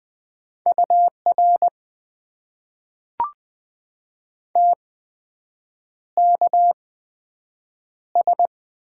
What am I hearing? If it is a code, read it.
Morse code, heard as a keyed tone, about 20 wpm, spaced as for 7 wpm, 700 Hz. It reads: UR TKS
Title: Top 90 QSO Elements - 8x Character Spacing - 20wpm